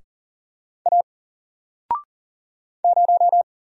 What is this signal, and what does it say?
Morse code, heard as a keyed tone, about 40 wpm, 700 Hz. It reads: A 0
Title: Single Character (Letter:Number:Punctuation:Prosign) 40wpm